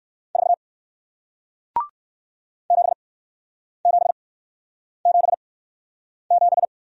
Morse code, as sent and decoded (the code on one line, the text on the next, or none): ....- / --... --... --... --...
4 7777